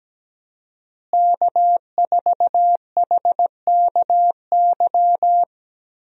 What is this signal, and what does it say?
Morse code, heard as a keyed tone, about 17 wpm, 700 Hz. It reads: K4HKY
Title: USA Call Signs - 5 Characters - Repeated 2x - 17wpm